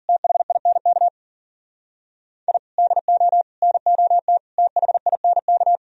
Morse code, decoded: THINK I DO NOT THINK

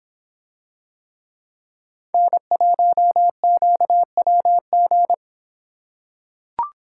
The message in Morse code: N1QWG

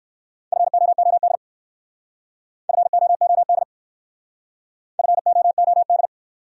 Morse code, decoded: FOOD FOOD FOOD

